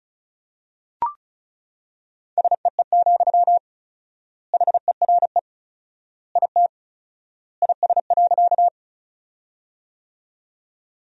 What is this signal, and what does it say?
Morse code, heard as a keyed tone, about 36 wpm, 700 Hz. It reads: SEE, HERE IT IS.
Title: 4x Word-Spacing - Sentences from Top 200 Words - 35wpm